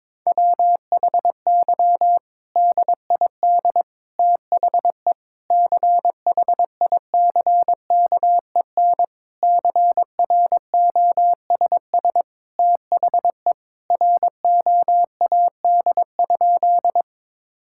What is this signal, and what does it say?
Morse code, heard as a keyed tone, about 22 wpm, 700 Hz. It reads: WHY DID THE CHICKEN CROSS THE ROAD?
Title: Sentences from Top 1000 Words - Review Part 5 - 22wpm